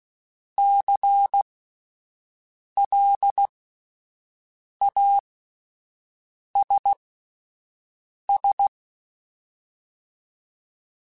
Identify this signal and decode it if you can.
Morse code, heard as a keyed tone, about 16 wpm, spaced as for 5 wpm, 800 Hz. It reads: CLASS
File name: QSOs Spaced